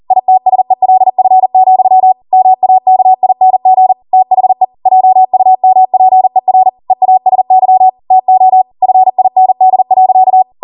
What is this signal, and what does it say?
Morse code, heard as a keyed tone, about 40 wpm, 750 Hz. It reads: ITSELF, MAKING THE JUMPER EASY TO FIND.